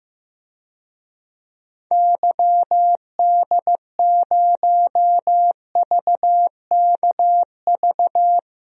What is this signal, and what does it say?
Morse code, heard as a keyed tone, about 15 wpm, 700 Hz. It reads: YD0VKV
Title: DXCC-100 Call Signs - 6 Characters - 15wpm